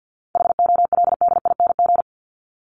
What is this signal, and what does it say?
Morse code, heard as a keyed tone, about 49 wpm, 700 Hz. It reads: HOLDING
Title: ING Suffix - Top 100 Words with Base Words 50wpm